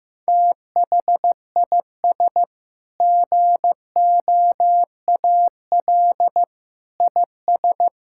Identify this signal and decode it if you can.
Morse code, heard as a keyed tone, about 15 wpm, 700 Hz. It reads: THIS GOAL IS
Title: Sentences from Top 1000 Words - Review Part 5 - 15wpm